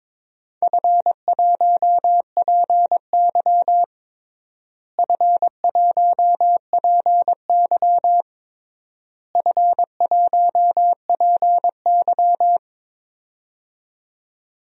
Morse code, heard as a keyed tone, about 22 wpm, 700 Hz. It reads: F1PY F1PY F1PY
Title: DXCC-100 Call Signs - 4 Characters - Repeated 3x - 22wpm